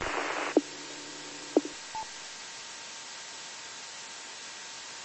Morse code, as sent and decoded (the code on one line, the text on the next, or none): none